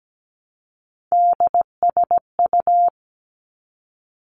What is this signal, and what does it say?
Morse code, heard as a keyed tone, about 17 wpm, 700 Hz. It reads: DSU
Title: Three Challenging Letter-Number Combinations 17wpm